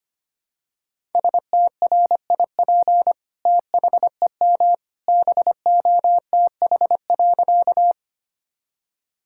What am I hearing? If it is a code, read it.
Morse code, heard as a keyed tone, about 25 wpm, 700 Hz. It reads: STRIP THEM BOTH.
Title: Sentences from Top 1600 Words 25wpm